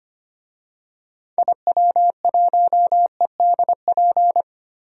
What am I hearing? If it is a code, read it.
Morse code, heard as a keyed tone, about 25 wpm, 700 Hz. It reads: IW1EDP
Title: DXCC-100 Call Signs - 6 Characters - Repeated 3x - 25wpm